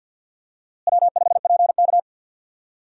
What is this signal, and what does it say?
Morse code, heard as a keyed tone, about 52 wpm, 700 Hz. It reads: W5PK
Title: USA Call Signs - 4 Characters - Repeated 2x - 50wpm